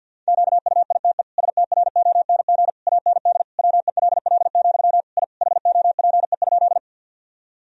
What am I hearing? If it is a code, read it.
Morse code, heard as a keyed tone, about 51 wpm, 700 Hz. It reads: QUITE STRONG AND WELL, I HOPE?
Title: Sentences from Top 1000 Words - Review Part 3 - 50wpm